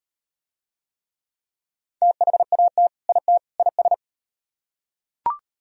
Morse code, heard as a keyed tone, about 38 wpm, 700 Hz. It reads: THAT IT IS